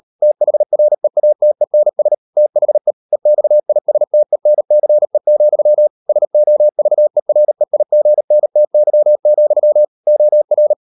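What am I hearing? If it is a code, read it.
Morse code, heard as a keyed tone, about 38 wpm, 600 Hz. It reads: THREATENS THE EXISTENCE, SOVEREIGNTY, OR
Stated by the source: Morse code practice at 35–39 wpm sending English text of Ken Follett's novel Never, from CW med Gnister.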